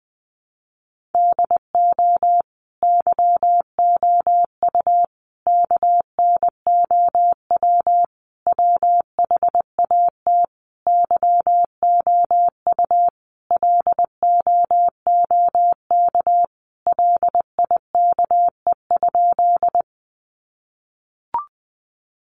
Morse code, heard as a keyed tone, about 20 wpm, 700 Hz. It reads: DO YOU KNOW WHAT YOU LOOK LIKE?
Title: Sentences from Top 100 Words 20wpm